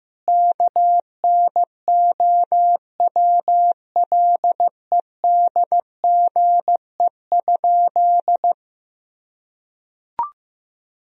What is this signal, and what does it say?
Morse code, heard as a keyed tone, about 15 wpm, 700 Hz. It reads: KNOWLEDGE?
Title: Sentences from Top 1000 Words - Review Part 3 - 15wpm